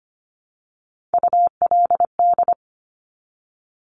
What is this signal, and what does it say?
Morse code, heard as a keyed tone, about 25 wpm, 700 Hz. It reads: ULD
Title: Top Three Letter Combinations in Words 25wpm